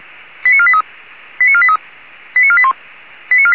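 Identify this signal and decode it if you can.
Five-tone selective call (ZVEI), heard as quick runs of five tones, tones of about 70 ms. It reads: 89483 89483 89481
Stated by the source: Received on SDR as ZVEI five-tone selective call.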